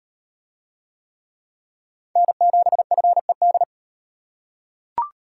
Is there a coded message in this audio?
Yes: Morse code from 2.2 to 3.6 s, heard as a keyed tone, about 38 wpm, 700 Hz. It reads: N7FED